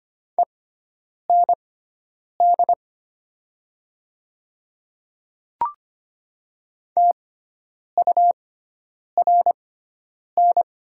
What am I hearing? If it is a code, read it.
Morse code, heard as a keyed tone, about 25 wpm, spaced as for 9 wpm, 700 Hz. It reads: END TURN
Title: 6x Character Spacing - Top 300 Words - 25wpm